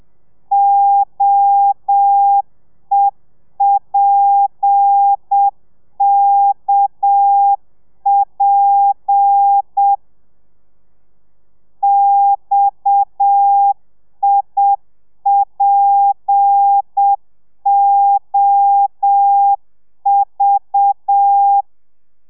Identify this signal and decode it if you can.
Morse code, heard as a keyed tone, about 7 wpm, 800 Hz. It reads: OEPKP XIPOV
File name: telegrafia / i2tfj / Lez7 30